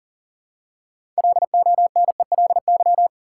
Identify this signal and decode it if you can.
Morse code, heard as a keyed tone, about 40 wpm, 700 Hz. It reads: LONELY
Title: Six Letter Words - 40wpm